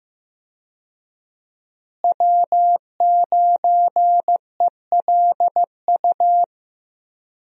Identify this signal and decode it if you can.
Morse code, heard as a keyed tone, about 15 wpm, 700 Hz. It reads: W9ELU
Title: USA Call Signs - 5 Characters - Repeated 2x - 15wpm